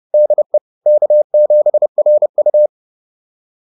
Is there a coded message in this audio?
Yes: Morse code from 0.1 to 2.7 s, heard as a keyed tone, about 30 wpm, 600 Hz. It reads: DE K7RU